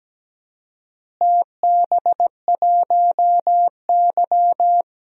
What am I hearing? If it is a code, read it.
Morse code, heard as a keyed tone, about 17 wpm, 700 Hz. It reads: TB1Y
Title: DXCC-100 Call Signs - 4 Characters - Repeated 3x - 17wpm